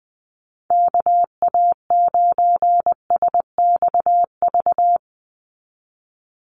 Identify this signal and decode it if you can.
Morse code, heard as a keyed tone, about 20 wpm, 700 Hz. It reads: KA9SXV
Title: Call Signs Repeated 3x - 20wpm